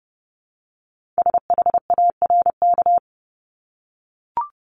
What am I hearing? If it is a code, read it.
Morse code, heard as a keyed tone, about 30 wpm, 700 Hz. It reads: SHARK
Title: Multi-letter Phonemes Repeated 3x 30wpm